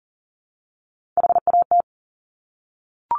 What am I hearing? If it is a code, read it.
Morse code, heard as a keyed tone, about 41 wpm, 700 Hz. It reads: HAT
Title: Top Three Letter Combinations in Words 40wpm